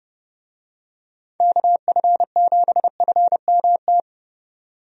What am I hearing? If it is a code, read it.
Morse code, heard as a keyed tone, about 30 wpm, 700 Hz. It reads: KF7FMT
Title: USA Call Signs - 6 Characters - 30wpm